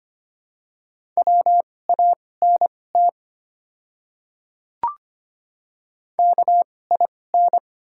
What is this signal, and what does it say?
Morse code, heard as a keyed tone, about 25 wpm, spaced as for 18 wpm, 700 Hz. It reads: WANT KIN
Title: 2x Character Spacing - Top 300 Words - 25wpm